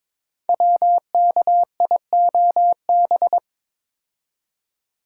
Morse code, heard as a keyed tone, about 22 wpm, 700 Hz. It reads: WKIOB
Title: Code Groups - 5 Letters - 22wpm